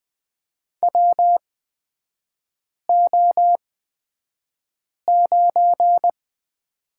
Morse code, heard as a keyed tone, about 20 wpm, spaced as for 5 wpm, 690 Hz. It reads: WO9
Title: 229 Lesson 37e - Letter x - Three Characters 20wpm